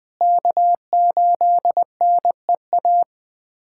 Morse code, heard as a keyed tone, about 21 wpm, 700 Hz. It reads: K8NEA
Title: USA Callsign DE Callsign 20wpm